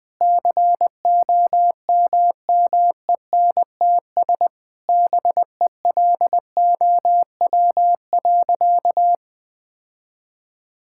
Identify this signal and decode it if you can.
Morse code, heard as a keyed tone, about 21 wpm, 700 Hz. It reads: COMMENTS BELOW.
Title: Sentences with 9-Word Phrases - Spoken Beginning - 20wpm